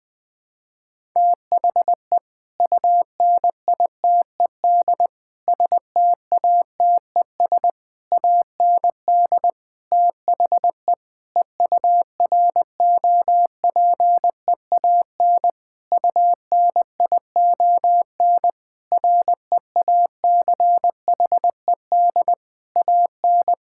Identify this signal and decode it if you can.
Morse code, heard as a keyed tone, about 20 wpm, 700 Hz. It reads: THE UNITED STATES AND THE EUROPEAN UNION REACHED AN
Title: Sentences with 5-Word Phrases - Spoken Beginning - 20wpm